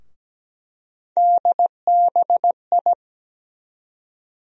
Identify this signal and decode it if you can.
Morse code, heard as a keyed tone, about 17 wpm, 700 Hz. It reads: DBI
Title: USA Call Sign Postfixes - 3 Letters 17wpm